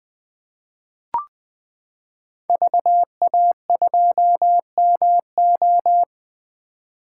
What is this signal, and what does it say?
Morse code, heard as a keyed tone, about 20 wpm, 700 Hz. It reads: VA2MO
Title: 2023 SOTA Chaser Call Signs - 20wpm